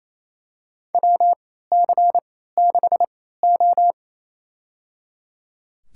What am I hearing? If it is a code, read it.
Morse code, heard as a keyed tone, about 27 wpm, spaced as for 16 wpm, 700 Hz. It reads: WC6O
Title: USA Call Signs - 3x Character Spacing - 28wpm